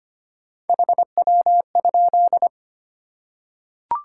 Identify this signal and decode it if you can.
Morse code, heard as a keyed tone, about 25 wpm, 700 Hz. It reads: HW?
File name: Top 20 QSO Elements - 25wpm